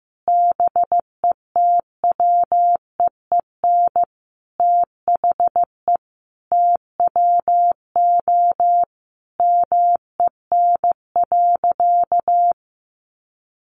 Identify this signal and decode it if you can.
Morse code, heard as a keyed tone, about 15 wpm, 700 Hz. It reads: BETWEEN THE TWO MEN.
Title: Sentences from Top 1300 Words 15wpm